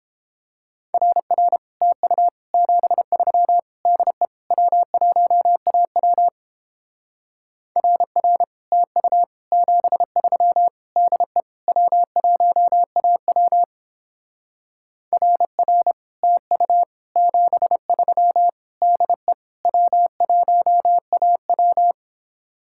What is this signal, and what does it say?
Morse code, heard as a keyed tone, about 29 wpm, 700 Hz. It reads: RR TU 73 DE W1AW RR TU 73 DE W1AW RR TU 73 DE W1AW